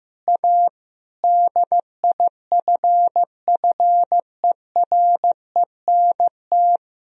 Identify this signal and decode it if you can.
Morse code, heard as a keyed tone, about 15 wpm, 700 Hz. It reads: A DIFFERENT
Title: Sentences from Top 1000 Words - Review Part 3 - 15wpm